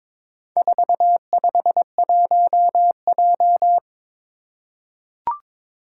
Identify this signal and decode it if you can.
Morse code, heard as a keyed tone, about 23 wpm, 700 Hz. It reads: 451J
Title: Four Challenging Letter-Number Combinations 22wpm